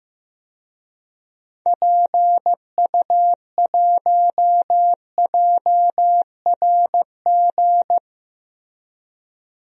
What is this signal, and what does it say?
Morse code, heard as a keyed tone, about 15 wpm, 700 Hz. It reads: PU1JRG
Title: DXCC-100 Call Signs - 6 Characters - Repeated 2x - 15wpm